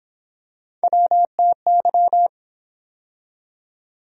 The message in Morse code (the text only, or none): WTY